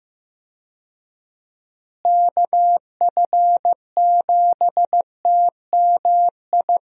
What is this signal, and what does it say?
Morse code, heard as a keyed tone, about 15 wpm, 700 Hz. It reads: KF7TMI
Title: USA Call Signs - 6 Characters - Repeated 3x - 15wpm